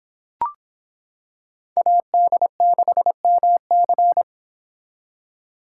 Morse code, heard as a keyed tone, about 25 wpm, 700 Hz. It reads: AD6MC